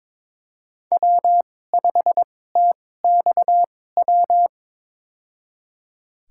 Morse code, heard as a keyed tone, about 22 wpm, spaced as for 16 wpm, 700 Hz. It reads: W5TXW
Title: USA Call Signs - 5 Characters - 2x Character Spacing - 22wpm